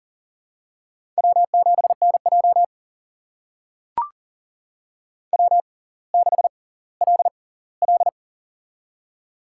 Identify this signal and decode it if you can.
Morse code, heard as a keyed tone, about 40 wpm, spaced as for 20 wpm, 700 Hz. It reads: W7NJ W6LL
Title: USA Call Signs - 6x Character Spacing - 40wpm